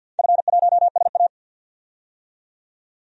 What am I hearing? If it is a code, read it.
Morse code, heard as a keyed tone, about 47 wpm, 700 Hz. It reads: F1SA
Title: DXCC-100 Call Signs - 4 Characters - 50wpm